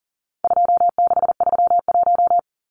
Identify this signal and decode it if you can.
Morse code, heard as a keyed tone, about 40 wpm, 700 Hz. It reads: 2631
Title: Numbers - 4-Digit 40wpm